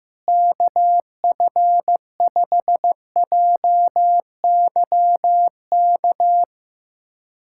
Morse code, heard as a keyed tone, about 15 wpm, 700 Hz. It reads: KF5JYK
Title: USA Callsign DE Callsign 15wpm